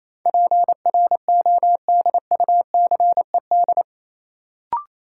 Morse code, heard as a keyed tone, about 28 wpm, 700 Hz. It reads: PRODUCED
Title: Sentences from Top 1000 Words - Review Part 6 - New Word Drill - 28wpm